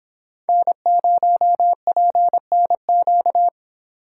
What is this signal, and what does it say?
Morse code, heard as a keyed tone, about 26 wpm, 700 Hz. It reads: N0PNQ